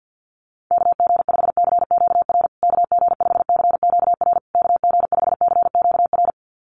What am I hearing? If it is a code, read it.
Morse code, heard as a keyed tone, about 50 wpm, 700 Hz. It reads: KG5CQR KG5CQR KG5CQR
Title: USA Call Signs Repeated 3x 50wpm